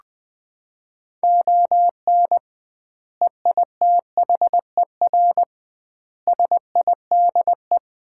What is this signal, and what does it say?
Morse code, heard as a keyed tone, about 21 wpm, 700 Hz. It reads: ON EITHER SIDE
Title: 2x Word-Spacing - Sets of 5 Words - 20wpm